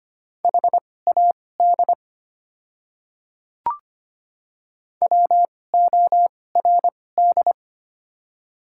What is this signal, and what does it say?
Morse code, heard as a keyed tone, about 25 wpm, spaced as for 18 wpm, 690 Hz. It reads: HAD WORD
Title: 2x Character Spacing - Top 100 Words - 25wpm